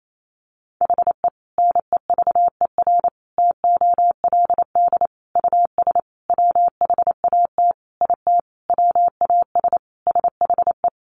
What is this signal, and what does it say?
Morse code, heard as a keyed tone, about 28 wpm, 700 Hz. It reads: HE NEVER TOLD US WHAT IT WAS SHE
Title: Sentences from Top 400 Words 28wpm